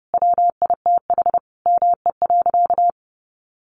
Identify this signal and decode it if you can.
Morse code, heard as a keyed tone, about 30 wpm, 700 Hz. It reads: WITH ME.
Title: Sentences from Top 1000 Words 30wpm